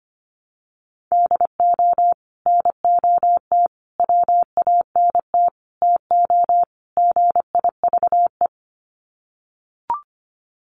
Morse code, heard as a keyed tone, about 25 wpm, 700 Hz. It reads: DO NOT WANT TO GIVE
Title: Sets of 5 Words Encore 25wpm